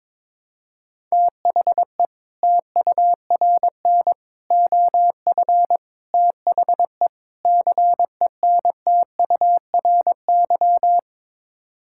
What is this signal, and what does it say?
Morse code, heard as a keyed tone, about 22 wpm, 700 Hz. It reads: THE TURN OF THE CENTURY
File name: practice / 22wpm / Build A Phrase - 5-Word Phrases - 22wpm